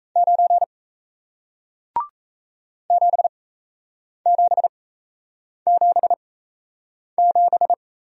Morse code, heard as a keyed tone, about 37 wpm, spaced as for 8 wpm, 700 Hz. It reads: 9 7777